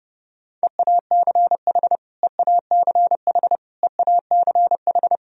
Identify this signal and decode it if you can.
Morse code, heard as a keyed tone, about 30 wpm, 700 Hz. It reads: EACH EACH EACH
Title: Top 100 Words Repeated 30wpm